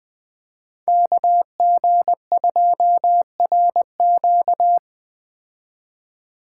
Morse code, heard as a keyed tone, about 20 wpm, 700 Hz. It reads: KG2RQ
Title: Speed Racing - USA 2x2 Call Signs 17wpm